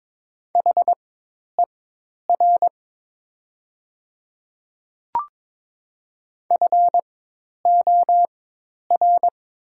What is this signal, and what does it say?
Morse code, heard as a keyed tone, about 22 wpm, spaced as for 10 wpm, 700 Hz. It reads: HER FOR
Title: 4x Character Spacing - Top 100 Words - 22wpm